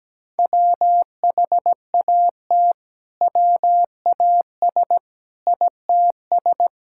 Morse code, heard as a keyed tone, about 17 wpm, 700 Hz. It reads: WHAT WAS ITS
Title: Sentences from Top 1000 Words - Review Part 5 - 17wpm